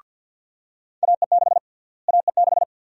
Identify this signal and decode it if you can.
Morse code, heard as a keyed tone, about 52 wpm, 700 Hz. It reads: AE6 AE6
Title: USA Call Sign Prefixes - Repeated 2x - 50wpm